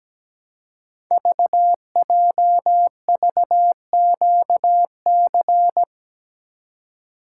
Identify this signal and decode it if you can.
Morse code, heard as a keyed tone, about 17 wpm, 700 Hz. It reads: VJVQC
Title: Code Groups - 5 Letters - 17wpm